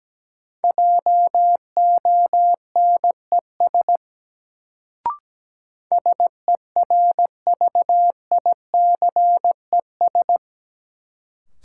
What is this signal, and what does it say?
Morse code, heard as a keyed tone, about 17 wpm, 700 Hz. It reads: JONES SERVICES
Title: Sets of 5 Words - New Word Drill - 17wpm